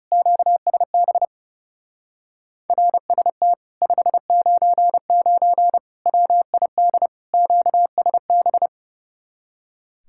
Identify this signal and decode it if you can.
Morse code, heard as a keyed tone, about 38 wpm, 700 Hz. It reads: QSB RST 599 WID QSB